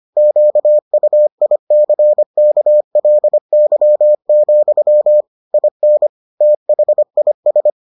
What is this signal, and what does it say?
Morse code, heard as a keyed tone, about 25 wpm, 600 Hz. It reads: QUICKLY, IN THIS